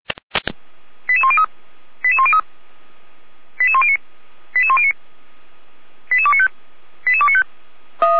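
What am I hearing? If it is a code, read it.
Five-tone selective call (ZVEI), heard as quick runs of five tones, tones of about 70 ms. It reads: 88103 88103 88109 88109 88295 88295
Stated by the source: ZVEI five-tone selective call received by SDR.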